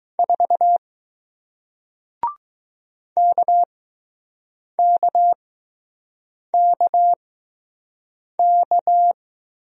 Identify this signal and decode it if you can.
Morse code, heard as a keyed tone, about 20 wpm, spaced as for 6 wpm, 700 Hz. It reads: 4 KKKK